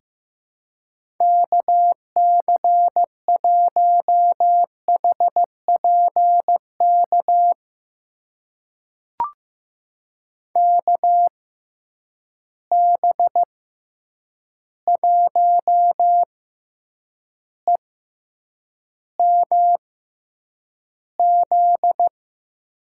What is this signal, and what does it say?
Morse code, heard as a keyed tone, about 15 wpm, spaced as for 8 wpm, 700 Hz. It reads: KC1HPK KB1EMZ
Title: USA Call Signs - 6 Characters - 6x Character Spacing - 15wpm